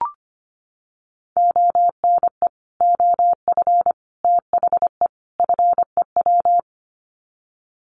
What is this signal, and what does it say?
Morse code, heard as a keyed tone, about 25 wpm, 700 Hz. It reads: ONE OF THE FEW
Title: Sets of 4 Words 25wpm